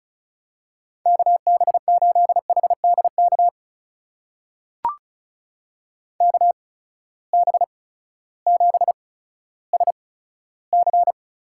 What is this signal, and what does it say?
Morse code, heard as a keyed tone, about 36 wpm, spaced as for 16 wpm, 700 Hz. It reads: KB8HDK KB7SC
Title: USA Call Signs - 8x Character Spacing - 35wpm